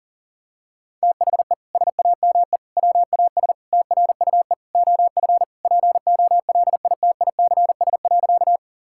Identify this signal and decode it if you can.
Morse code, heard as a keyed tone, about 40 wpm, 700 Hz. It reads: THE SAME WAS TRUE OF POLITICS.
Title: Sentences from Top 1000 Words - Review Part 6 - 40wpm